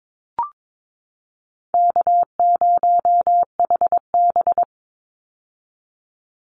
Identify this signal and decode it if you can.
Morse code, heard as a keyed tone, about 22 wpm, 700 Hz. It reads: K0HB
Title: Call Signs - 22wpm